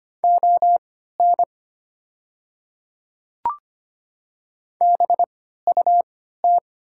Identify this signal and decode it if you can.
Morse code, heard as a keyed tone, about 25 wpm, spaced as for 14 wpm, 700 Hz. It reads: ON BUT
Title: 3x Character Spacing - Top 300 Words - 25wpm